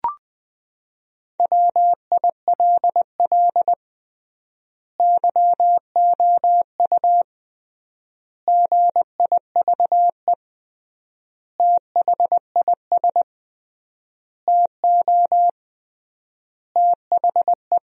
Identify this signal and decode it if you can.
Morse code, heard as a keyed tone, about 21 wpm, 700 Hz. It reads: WILL YOU GIVE THIS TO THE
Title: 3x Word-Spacing - Sentences from Top 200 Words - 20wpm